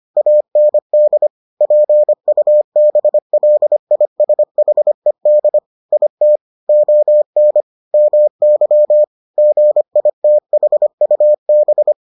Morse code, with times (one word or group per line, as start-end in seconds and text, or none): 0.2-1.3 AND
1.6-5.6 PUBLISHED
5.9-6.4 IT
6.7-7.6 ON
7.9-9.0 MY
9.4-11.9 GITHUB